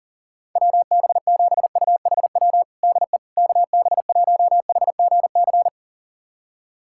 Morse code, heard as a keyed tone, about 37 wpm, 700 Hz. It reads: WB7UHW DE KB1HGC
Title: USA Callsign DE Callsign 40wpm